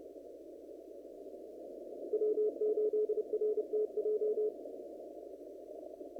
Morse code, heard as a keyed tone, about 30 wpm, 430 Hz. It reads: W8RTJ
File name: CW / kt5j wpx cw-op-k5pi